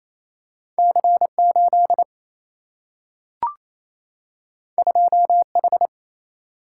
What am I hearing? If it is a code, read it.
Morse code, heard as a keyed tone, about 28 wpm, 700 Hz. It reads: C 8 2 H
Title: Single-Letter Single-Number 28wpm